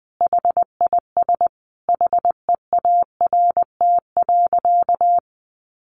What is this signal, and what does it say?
Morse code, heard as a keyed tone, about 19 wpm, 700 Hz. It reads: HIS HEART.
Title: Sentences from Top 1200 Words 20wpm